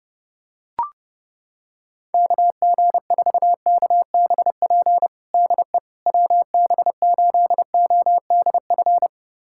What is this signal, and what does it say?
Morse code, heard as a keyed tone, about 32 wpm, 700 Hz. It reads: KG4KBP DE WB8ODF
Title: USA Callsign DE Callsign 30wpm